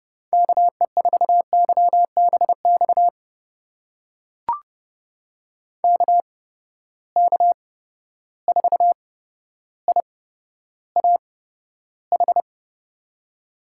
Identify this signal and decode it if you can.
Morse code, heard as a keyed tone, about 30 wpm, spaced as for 13 wpm, 700 Hz. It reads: KE4YBX KK4IAH
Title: USA Call Signs - 8x Character Spacing - 30wpm